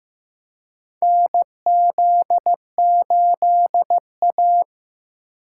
Morse code, heard as a keyed tone, about 15 wpm, 700 Hz. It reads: NZ8A
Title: USA Call Signs - 4 Characters - 2x Character Spacing - 15wpm